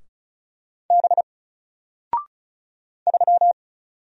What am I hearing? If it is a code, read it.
Morse code, heard as a keyed tone, about 36 wpm, 700 Hz. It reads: B 3
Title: Single Character (Letter:Number:Punctuation:Prosign) Alt 35wpm